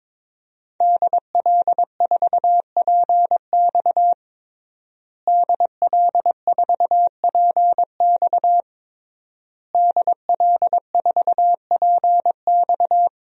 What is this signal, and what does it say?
Morse code, heard as a keyed tone, about 22 wpm, 700 Hz. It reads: DL4PX DL4PX DL4PX
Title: DXCC-100 Call Signs - 5 Characters - Repeated 3x - 22wpm